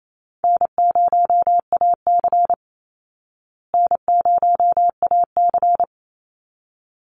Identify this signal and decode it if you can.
Morse code, heard as a keyed tone, about 28 wpm, 700 Hz. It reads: N0AC N0AC
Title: Call Signs Repeated 3x - 28wpm